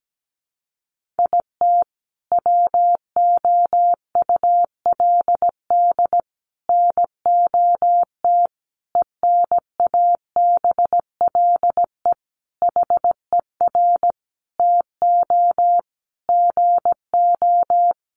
Morse code, heard as a keyed tone, about 17 wpm, 700 Hz. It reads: IT WOULD NOT ENABLE HER TO GO
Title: Sentences from Top 1500 Words 17wpm